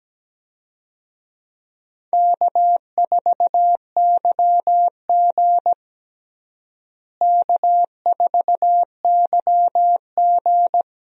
Morse code, heard as a keyed tone, about 17 wpm, 700 Hz. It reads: K4YG K4YG
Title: USA Call Signs - 4 Characters - Repeated 2x - 17wpm